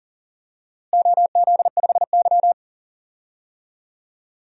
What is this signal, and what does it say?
Morse code, heard as a keyed tone, about 37 wpm, 700 Hz. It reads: OZ5Y
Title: DXCC-100 Call Signs - 4 Characters - Repeated 2x - 40wpm